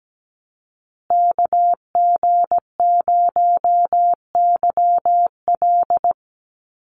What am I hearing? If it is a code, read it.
Morse code, heard as a keyed tone, about 17 wpm, 700 Hz. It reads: KG0YL
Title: Call Signs Repeated 2x - 17wpm